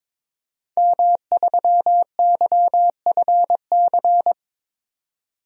Morse code, heard as a keyed tone, about 22 wpm, 700 Hz. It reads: M3YFC
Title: DXCC-100 Call Signs - 5 Characters - 22wpm